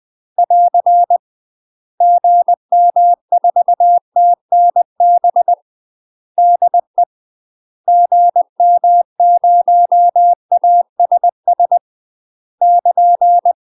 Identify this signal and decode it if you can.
Morse code, heard as a keyed tone, about 20 wpm, 700 Hz. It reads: <AR> GM4TNB DE GM0ASS <KN>
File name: UK MASTER QSOs TRAINING / 20WPM / UK MASTER QSOs 0062